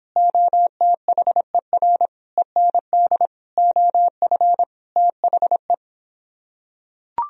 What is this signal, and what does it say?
Morse code, heard as a keyed tone, about 26 wpm, 700 Hz. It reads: OTHER END OF THE